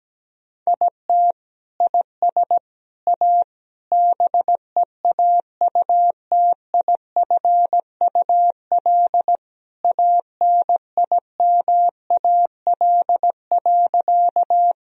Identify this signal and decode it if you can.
Morse code, heard as a keyed tone, about 17 wpm, 700 Hz. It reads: IT IS A BEAUTIFUL ANIMAL.
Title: Sentences from Top 1000 Words - Review Part 1 - 17wpm